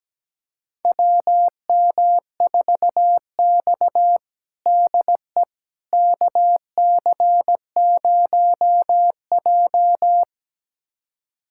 Morse code, heard as a keyed tone, about 17 wpm, 700 Hz. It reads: WM4X DE KC0J
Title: USA Callsign DE Callsign - 4 Characters - 17wpm